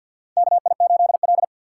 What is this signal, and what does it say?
Morse code, heard as a keyed tone, about 53 wpm, 700 Hz. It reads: KI8L
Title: USA Call Signs - 4 Characters - 6x Character Spacing - 50wpm